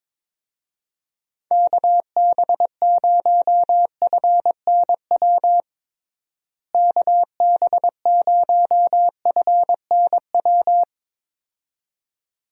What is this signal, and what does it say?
Morse code, heard as a keyed tone, about 21 wpm, 700 Hz. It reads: KB0FNW KB0FNW
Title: USA Call Signs - 6 Characters - Repeated 2x - 22wpm